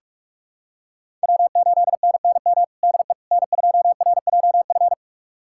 Morse code, heard as a keyed tone, about 42 wpm, 700 Hz. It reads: W8NNM DE N2RJF
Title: USA Callsign DE Callsign - 5 Characters - 45wpm